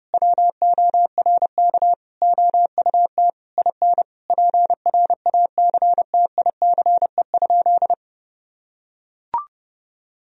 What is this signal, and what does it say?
Morse code, heard as a keyed tone, about 30 wpm, 700 Hz. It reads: WORK OUT IN PRACTICE?
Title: Sentences from Top 1000 Words - Review Part 4 - 30wpm